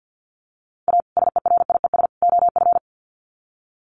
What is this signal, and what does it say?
Morse code, heard as a keyed tone, about 50 wpm, 700 Hz. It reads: A SERIES OF
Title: Sets of 3 Words 50wpm